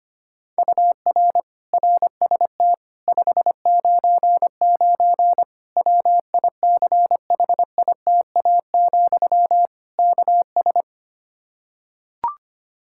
Morse code, heard as a keyed tone, about 25 wpm, 700 Hz. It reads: UR RST 599 WICHITA, KS
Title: Practice QSO - Exchange 1 - Beginning - 25wpm